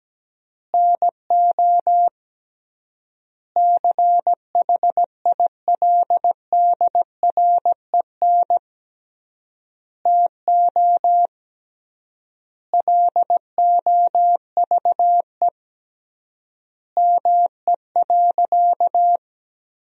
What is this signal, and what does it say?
Morse code, heard as a keyed tone, about 17 wpm, 700 Hz. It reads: NO CHILDREN TO LOVE ME.
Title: 3x Word-Spacing - Sentences from Top 400 Words - 17wpm